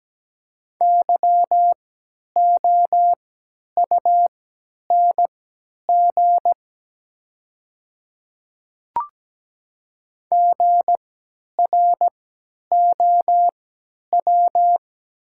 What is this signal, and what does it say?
Morse code, heard as a keyed tone, about 17 wpm, spaced as for 10 wpm, 700 Hz. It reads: YOUNG GROW
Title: 3x Character Spacing - Top 500 Words - 17wpm